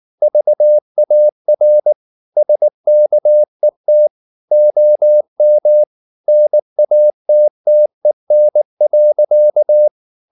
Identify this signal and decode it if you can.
Morse code, heard as a keyed tone, about 19 wpm, 600 Hz. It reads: VAR SKET OM NATTEN.